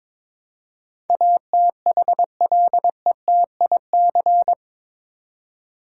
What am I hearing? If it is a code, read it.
Morse code, heard as a keyed tone, about 22 wpm, 700 Hz. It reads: ATHLETIC
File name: practice / 22wpm / Eight Letter Words - 22wpm